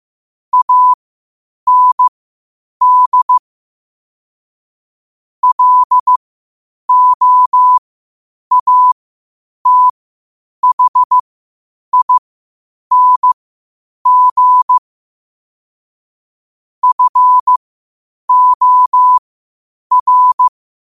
Morse code, heard as a keyed tone, about 15 wpm, spaced as for 8 wpm, 1000 Hz. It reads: AND LOATHING FOR
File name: qotdfiles / 8wpm